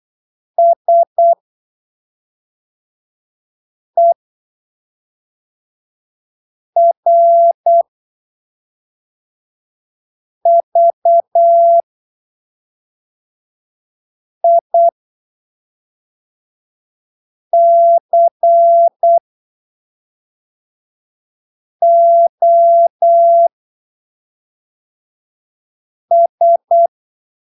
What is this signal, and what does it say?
Morse code, heard as a keyed tone, about 8 wpm, spaced as for 3 wpm, 680 Hz. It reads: SERVICOS
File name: arquivos / texto / texto 12 08ppm